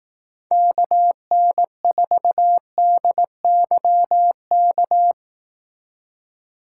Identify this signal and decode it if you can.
Morse code, heard as a keyed tone, about 18 wpm, 700 Hz. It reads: KN4DYK
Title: Speed Racing - USA Call Signs 15wpm